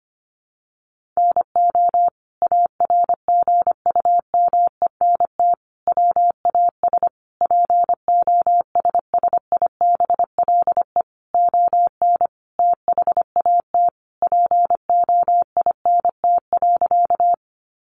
Morse code, heard as a keyed tone, about 25 wpm, 700 Hz. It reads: NO ARGUMENT WAS POSSIBLE ON THAT POINT.
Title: Sentences from Top 1400 Words 25wpm